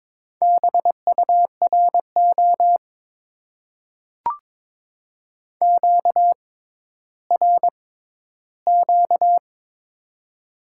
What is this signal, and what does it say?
Morse code, heard as a keyed tone, about 22 wpm, spaced as for 13 wpm, 700 Hz. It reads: BURO QRQ
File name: Top 90 QSO Elements - 6x Character Spacing - 22wpm